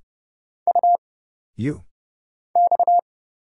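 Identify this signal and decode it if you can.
Morse code, heard as a keyed tone, about 30 wpm, 700 Hz. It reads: U X
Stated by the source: Morse Code Ninja single-character practice (letters, numbers, punctuation, prosigns) at 30 wpm, by Kurt Zoglmann (AD0WE).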